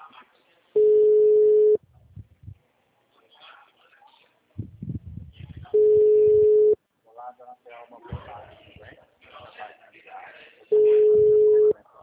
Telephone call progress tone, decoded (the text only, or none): ringback tone